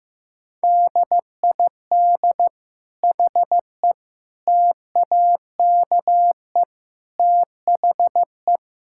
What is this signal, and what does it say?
Morse code, heard as a keyed tone, about 16 wpm, 700 Hz. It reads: DID HE TAKE THE
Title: Sentences from Top 1000 Words - Review Part 6 - 15wpm